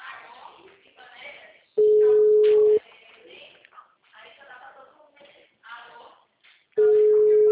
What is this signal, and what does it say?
Ringback tone (Europe, 425 Hz), with speech.
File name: out- -2010- - - .13916